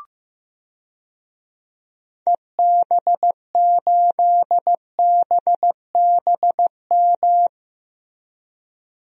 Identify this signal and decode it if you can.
Morse code, heard as a keyed tone, about 15 wpm, 700 Hz. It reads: EB8BBM
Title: DXCC-100 Call Signs - 6 Characters - Repeated 2x - 15wpm